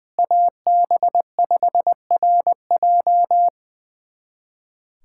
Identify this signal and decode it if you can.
Morse code, heard as a keyed tone, about 20 wpm, 700 Hz. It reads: AB5RJ
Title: USA Callsign DE Callsign 20wpm